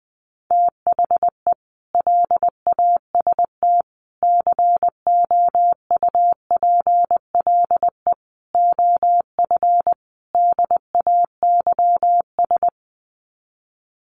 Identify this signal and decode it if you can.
Morse code, heard as a keyed tone, about 20 wpm, 700 Hz. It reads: THE LAST COUPLE OF DAYS
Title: Sets of 5 Words Encore 20wpm